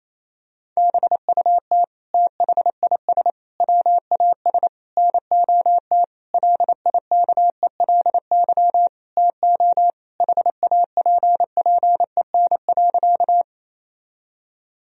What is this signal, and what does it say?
Morse code, heard as a keyed tone, about 28 wpm, 700 Hz. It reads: BUT THIS WAS NOT LIKELY TO HAPPEN.
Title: Sentences from Top 1000 Words - Review Part 4 - 28wpm